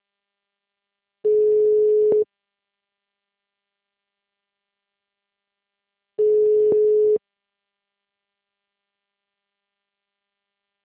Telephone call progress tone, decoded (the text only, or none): ringback tone